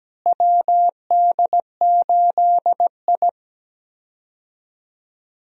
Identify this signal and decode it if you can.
Morse code, heard as a keyed tone, about 17 wpm, 700 Hz. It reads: WD8I
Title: USA Call Signs - 4 Characters - Repeated 2x - 17wpm